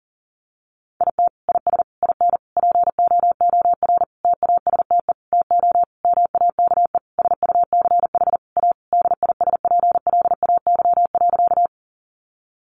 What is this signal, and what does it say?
Morse code, heard as a keyed tone, about 37 wpm, 700 Hz. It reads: IT IS IN POOR TASTE TO MAKE SUCH A DISPLAY.
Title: Sentences from Top 1200 Words 40wpm